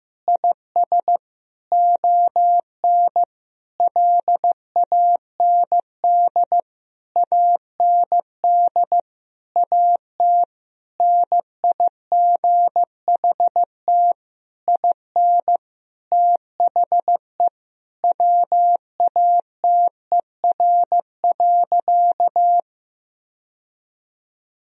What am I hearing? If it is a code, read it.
Morse code, heard as a keyed tone, about 15 wpm, 700 Hz. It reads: IS ON LAND AND AT NIGHT IN THE WATER.
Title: Sentences from Top 1000 Words - Review Part 1 - 15wpm